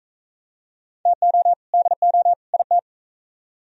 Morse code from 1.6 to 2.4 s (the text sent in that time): DO